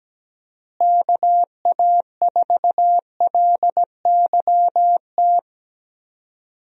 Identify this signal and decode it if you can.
Morse code, heard as a keyed tone, about 17 wpm, 700 Hz. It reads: KA4LYT
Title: USA Call Signs - 6 Characters - Repeated 3x - 17wpm